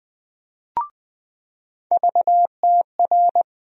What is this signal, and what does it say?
Morse code, heard as a keyed tone, about 20 wpm, 700 Hz. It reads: VTR
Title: USA Call Sign Postfixes - 3 Letters 20wpm